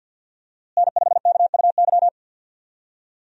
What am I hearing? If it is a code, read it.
Morse code, heard as a keyed tone, about 49 wpm, 700 Hz. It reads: N5KUY
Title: USA Call Signs - 5 Characters - Repeated 3x - 50wpm